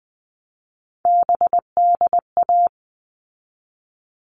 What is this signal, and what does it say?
Morse code, heard as a keyed tone, about 20 wpm, 700 Hz. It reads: BDA
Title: All Cut Numbers - 3-Digit 20wpm